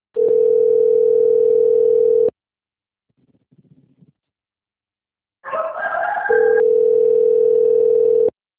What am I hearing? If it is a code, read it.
Ringback tone (North America), 440 + 480 Hz.
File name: rg-1600-unknown- - - .52271